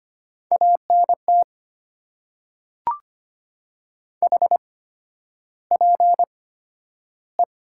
Morse code, heard as a keyed tone, about 25 wpm, spaced as for 11 wpm, 700 Hz. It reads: ANT HPE
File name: Top 90 QSO Elements - 8x Character Spacing - 25wpm